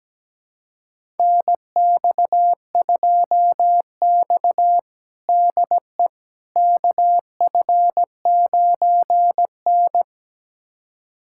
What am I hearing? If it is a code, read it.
Morse code, heard as a keyed tone, about 17 wpm, 700 Hz. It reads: NX2X DE KF9N
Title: USA Callsign DE Callsign - 4 Characters - 17wpm